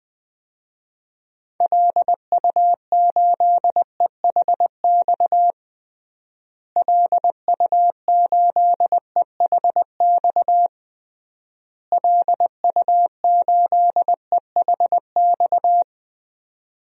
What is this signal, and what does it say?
Morse code, heard as a keyed tone, about 20 wpm, 700 Hz. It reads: LU8EHX LU8EHX LU8EHX
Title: DXCC-100 Call Signs - 6 Characters - Repeated 3x - 20wpm